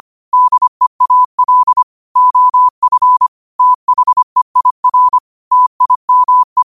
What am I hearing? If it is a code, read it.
Morse code, heard as a keyed tone, about 25 wpm, 1000 Hz. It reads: DEAL OF THEIR TIME
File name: qotdfiles / 25wpm